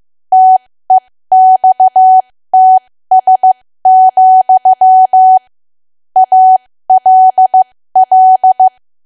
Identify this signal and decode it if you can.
Morse code, heard as a keyed tone, about 15 wpm, spaced as for 13 wpm, 750 Hz. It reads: TEXTS, ALL
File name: cwa / W1AW-Level2 / 13WPM